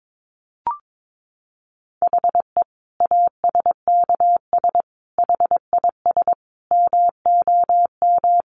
Morse code, heard as a keyed tone, about 22 wpm, 700 Hz. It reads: HE ASKS HIS MOM
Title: Easy Phrases 22wpm